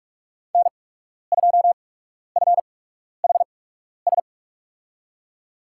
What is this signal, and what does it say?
Morse code, heard as a keyed tone, about 44 wpm, spaced as for 12 wpm, 690 Hz. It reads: N2FHS
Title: USA Call Signs - 5 Characters - 8x Character Spacing - 45wpm